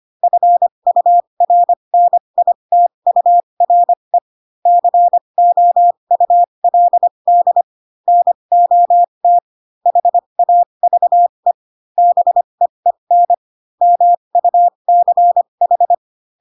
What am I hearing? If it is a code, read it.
Morse code, heard as a keyed tone, about 25 wpm, spaced as for 22 wpm, 700 Hz. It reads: FURNITURE COULD NOT HAVE BEEN MUCH
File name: Pinocchio / Pinocchio ch3 18 25 ik1hge